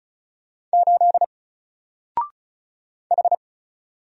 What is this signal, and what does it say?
Morse code, heard as a keyed tone, about 36 wpm, 700 Hz. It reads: 8 H
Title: Single Letter-Number Alt 35wpm